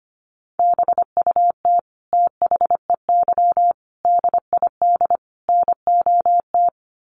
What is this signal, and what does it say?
Morse code, heard as a keyed tone, about 25 wpm, 700 Hz. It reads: BUT THEY DID NOT
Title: Sentences from Top 1400 Words 25wpm